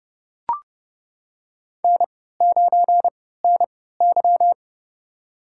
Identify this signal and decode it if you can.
Morse code, heard as a keyed tone, about 30 wpm, spaced as for 17 wpm, 700 Hz. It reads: N9NY
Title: USA Call Signs - 4 Characters - 3x Character Spacing - 30wpm